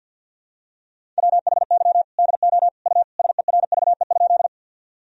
Morse code, heard as a keyed tone, about 53 wpm, 700 Hz. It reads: WHY DO U SERVE?
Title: Sentences from Top 400 Words and Abbreviations 50wpm